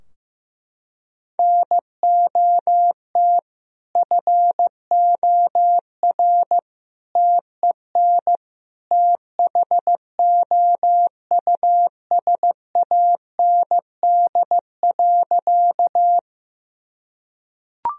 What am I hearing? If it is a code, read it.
Morse code, heard as a keyed tone, about 15 wpm, 700 Hz. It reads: NOT FOR TEN THOUSAND.
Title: Sentences from Top 1000 Words - Review Part 3 - 15wpm